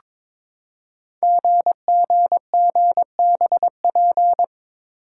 Morse code, heard as a keyed tone, about 23 wpm, 700 Hz. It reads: GGGBP
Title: Code Groups - 5 Letters - 22wpm